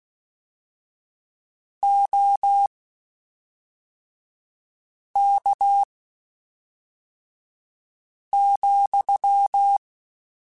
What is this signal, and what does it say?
Morse code, heard as a keyed tone, about 16 wpm, spaced as for 3 wpm, 800 Hz. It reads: OK,
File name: (spaced) / QSOs Spaced-mp3s / kc7sac @5 sp